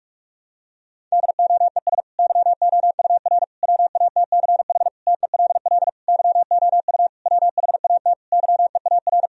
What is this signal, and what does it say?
Morse code, heard as a keyed tone, about 41 wpm, 700 Hz. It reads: DOES YOUR WATCH TELL YOU WHAT YEAR
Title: Sentences from Top 1000 Words - Review Part 1 - 45wpm